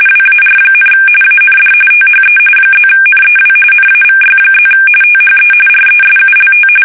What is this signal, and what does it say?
RTTY, about 45.45 baud. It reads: WAR, AND THE +NEW NAVY+ THE RESULT
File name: fsk850 / usnavy1